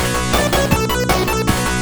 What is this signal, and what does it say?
There is no code.